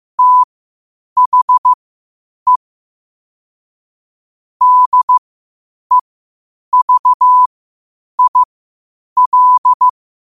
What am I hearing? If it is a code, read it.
Morse code, heard as a keyed tone, about 15 wpm, spaced as for 8 wpm, 1000 Hz. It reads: THE DEVIL
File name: qotdfiles / 8wpm